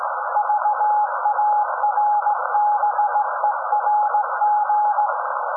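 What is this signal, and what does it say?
Morse code, heard as a keyed tone, about 17 wpm, 840 Hz. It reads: VARBERG
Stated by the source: SAQ Grimeton Radio, Alexanderson 200 kW alternator on 17.2 kHz, received in Potsdam with a very strong signal on World Radio Day.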